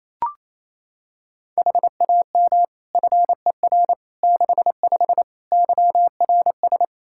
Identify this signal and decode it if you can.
Morse code, heard as a keyed tone, about 28 wpm, 700 Hz. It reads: HAM FER 65 YRS
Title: Practice QSO - Exchange 3 - More Info - 28wpm